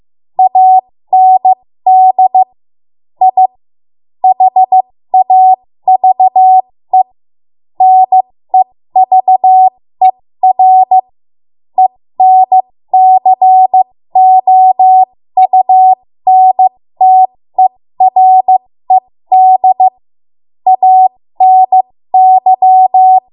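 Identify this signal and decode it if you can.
Morse code, heard as a keyed tone, about 15 wpm, spaced as for 13 wpm, 750 Hz. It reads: AND I HAVE NEVER ENCOUNTERED ANY
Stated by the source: ARRL Morse code practice at 13 WPM.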